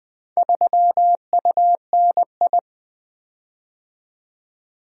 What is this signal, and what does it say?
Morse code, heard as a keyed tone, about 20 wpm, 700 Hz. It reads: 3UNI
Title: Four Letter-Number Combinations - 20wpm